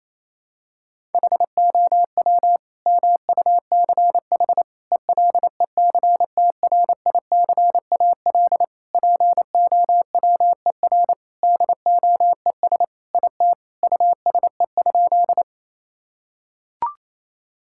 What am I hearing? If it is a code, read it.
Morse code, heard as a keyed tone, about 28 wpm, 700 Hz. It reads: HOW MUCH ELECTRICAL POWER DOES IT USE?
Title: Sentences from Top 1000 Words - Review Part 6 - 28wpm